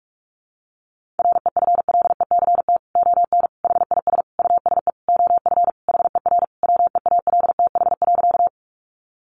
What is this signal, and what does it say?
Morse code, heard as a keyed tone, about 44 wpm, 700 Hz. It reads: REFLECT ON HIS USE OF HER WEALTH.
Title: Sentences from Top 1500 Words 45wpm